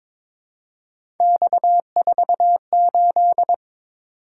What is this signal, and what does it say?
Morse code, heard as a keyed tone, about 22 wpm, 700 Hz. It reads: X48
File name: practice / 22wpm / Three Letter-Number Combinations - 22wpm